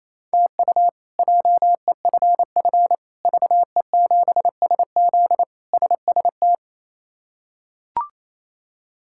Morse code, heard as a keyed tone, about 27 wpm, 700 Hz. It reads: TU JEFF VE7SZ SST
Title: K1USN SST QSO Practice - 28wpm